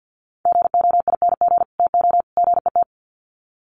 Morse code, heard as a keyed tone, about 50 wpm, 700 Hz. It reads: GOING TO GET
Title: Sets of 3 Words 50wpm